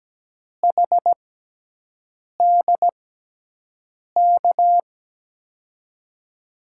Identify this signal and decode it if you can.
Morse code, heard as a keyed tone, about 17 wpm, spaced as for 6 wpm, 700 Hz. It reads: HDK
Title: USA Call Signs - 6x Character Spacing - 17wpm